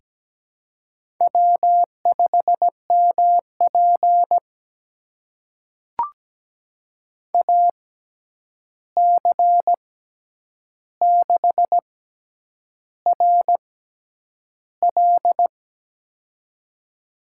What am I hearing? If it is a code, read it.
Morse code, heard as a keyed tone, about 17 wpm, spaced as for 8 wpm, 700 Hz. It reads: W5MP AC6RL
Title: USA Call Signs - 6x Character Spacing - 17wpm